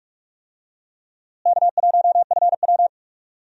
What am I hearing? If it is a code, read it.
Morse code, heard as a keyed tone, about 43 wpm, 700 Hz. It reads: K1FW
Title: USA Call Signs - 4 Characters - Repeated 3x - 45wpm